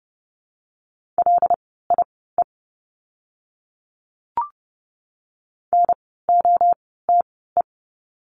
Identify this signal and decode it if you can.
Morse code, heard as a keyed tone, about 30 wpm, spaced as for 17 wpm, 700 Hz. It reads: LIE NOTE
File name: Lesson 16f - Letter c - Words 30wpm